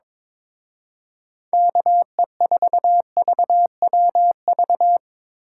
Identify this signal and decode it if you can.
Morse code, heard as a keyed tone, about 24 wpm, 700 Hz. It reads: KE4VWV
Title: Speed Racing - USA Call Signs 22wpm